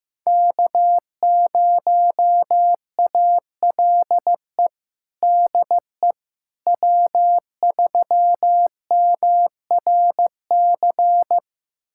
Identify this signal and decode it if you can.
Morse code, heard as a keyed tone, about 15 wpm, 700 Hz. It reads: K0ALE DE W3MRC
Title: USA Callsign DE Callsign - 5 Characters - 15wpm